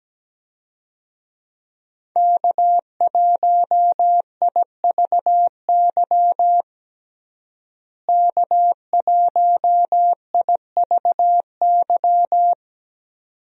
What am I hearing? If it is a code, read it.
Morse code, heard as a keyed tone, about 17 wpm, 700 Hz. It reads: K1IVY K1IVY
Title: USA Call Signs - 5 Characters - Repeated 2x - 17wpm